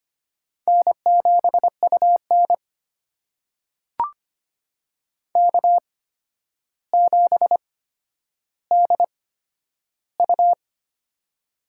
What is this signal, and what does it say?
Morse code, heard as a keyed tone, about 25 wpm, spaced as for 11 wpm, 700 Hz. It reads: N7UN K7DU
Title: USA Call Signs - 4 Characters - 8x Character Spacing - 25wpm